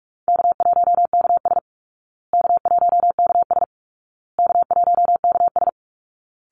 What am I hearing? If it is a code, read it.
Morse code, heard as a keyed tone, about 44 wpm, 700 Hz. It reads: K1KS K1KS K1KS
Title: Easy Call Signs Repeated 3x - 45wpm